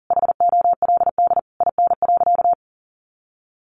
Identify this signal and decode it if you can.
Morse code, heard as a keyed tone, about 40 wpm, 700 Hz. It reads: HOLD IN.
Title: Sentences from Top 400 Words 40wpm